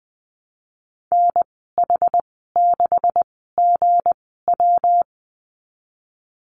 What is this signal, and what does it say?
Morse code, heard as a keyed tone, about 20 wpm, spaced as for 15 wpm, 700 Hz. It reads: NH6GW
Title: 193 Lesson 31g - Letter v - Call Signs 20wpm